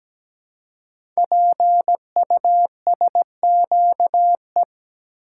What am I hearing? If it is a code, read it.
Morse code, heard as a keyed tone, about 17 wpm, 700 Hz. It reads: PUSQE
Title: Code Groups - 5 Letters - 17wpm